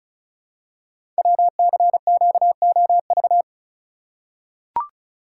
Morse code, heard as a keyed tone, about 36 wpm, 700 Hz. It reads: WCQOV
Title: Code Groups - 5 Letters - 35wpm